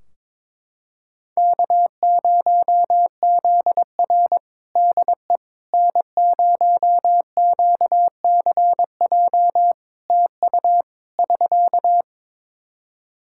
Morse code, heard as a keyed tone, about 22 wpm, 700 Hz. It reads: K0ZR DE N0QCJ TU <SK>
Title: Practice QSO - Exchange 4 - Ending - 22wpm